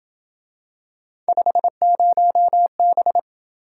Morse code, heard as a keyed tone, about 26 wpm, 700 Hz. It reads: 50B